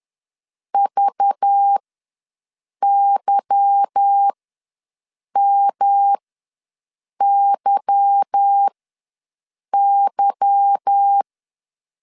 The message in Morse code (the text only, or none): VYMYY